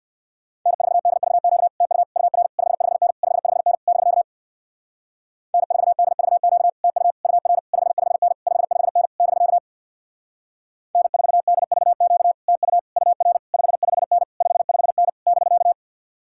Morse code, heard as a keyed tone, about 62 wpm, 700 Hz. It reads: N4DVQ TU UR 55N 55N <BK> N4DVQ TU UR 55N 55N <BK> N4DVQ TU UR 55N 55N <BK>
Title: Speed-Racing Common QSO Phrases Encore - 50wpm